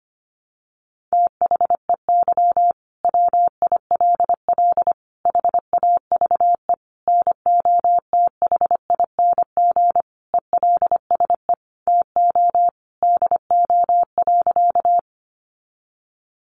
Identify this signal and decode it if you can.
Morse code, heard as a keyed tone, about 25 wpm, 700 Hz. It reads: THEY WILL HAVE NOTHING ELSE TO DO.
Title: Sentences from Top 800 Words 25wpm